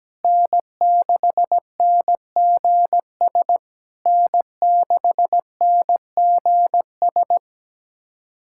Morse code, heard as a keyed tone, about 19 wpm, 700 Hz. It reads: N6NGS N6NGS
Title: SOTA QSO Practice 17wpm